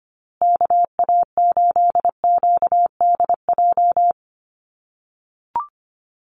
Morse code, heard as a keyed tone, about 25 wpm, 700 Hz. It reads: KA8QDJ
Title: USA Call Signs Repeated 3x 25wpm